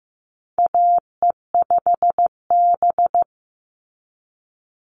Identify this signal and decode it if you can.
Morse code, heard as a keyed tone, about 15 wpm, 700 Hz. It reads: AE5B
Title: NA SOTA Chaser Callsigns 15wpm